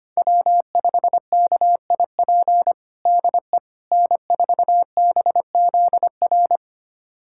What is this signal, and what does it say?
Morse code, heard as a keyed tone, about 25 wpm, 700 Hz. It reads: W5KIP DE N4BZR
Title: USA Callsign DE Callsign - 5 Characters - 25wpm